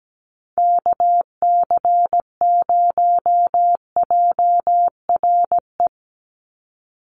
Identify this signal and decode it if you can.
Morse code, heard as a keyed tone, about 17 wpm, 700 Hz. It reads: KC0JRE
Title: USA Call Signs 17wpm